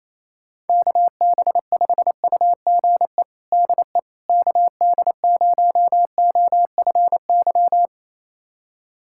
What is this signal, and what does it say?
Morse code, heard as a keyed tone, about 28 wpm, 700 Hz. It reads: KB5UGE DE KD0OFY
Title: USA Callsign DE Callsign 28wpm